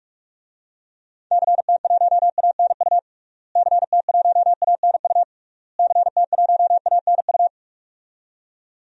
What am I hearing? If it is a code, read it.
Morse code, heard as a keyed tone, about 41 wpm, 700 Hz. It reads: CT1ANU CT1ANU CT1ANU
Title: DXCC-100 Call Signs - 6 Characters - Repeated 3x - 45wpm